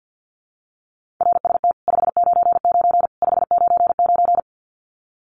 Morse code, heard as a keyed tone, about 49 wpm, 700 Hz. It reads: RST 599 599
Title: Common QSO Phrases 50wpm